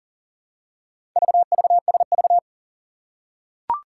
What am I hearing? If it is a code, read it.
Morse code, heard as a keyed tone, about 41 wpm, 700 Hz. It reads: VVSV
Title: Four Challenging Letter-Number Combinations 40wpm